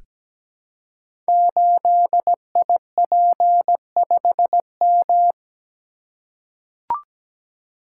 Morse code, heard as a keyed tone, about 17 wpm, 700 Hz. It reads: 8IP5M
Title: Code Groups - 5 Letter:Numbers - 17wpm